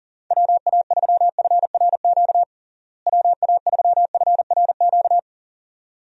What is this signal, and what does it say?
Morse code, heard as a keyed tone, about 40 wpm, 700 Hz. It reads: WA3FRQ WA3FRQ
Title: USA Call Signs - 6 Characters - Repeated 2x - 40wpm